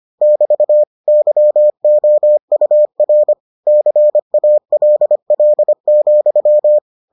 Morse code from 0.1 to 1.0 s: <BT>

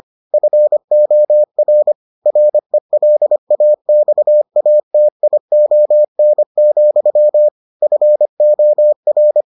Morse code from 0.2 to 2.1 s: FOR

